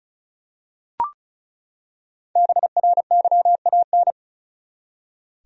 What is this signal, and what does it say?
Morse code, heard as a keyed tone, about 34 wpm, 700 Hz. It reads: BRYAN
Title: USA Common Names - 35wpm